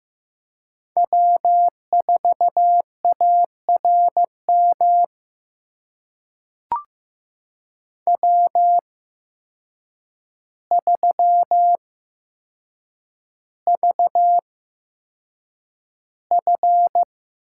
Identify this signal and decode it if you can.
Morse code, heard as a keyed tone, about 15 wpm, spaced as for 7 wpm, 700 Hz. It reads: W4ARM W3VF
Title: USA Call Signs - 5 Characters - 8x Character Spacing - 15wpm